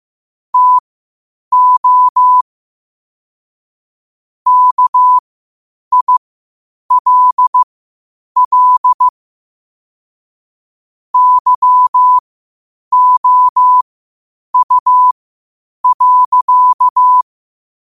Morse code, heard as a keyed tone, about 15 wpm, spaced as for 8 wpm, 1000 Hz. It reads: TO KILL YOU.
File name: qotdfiles / 8wpm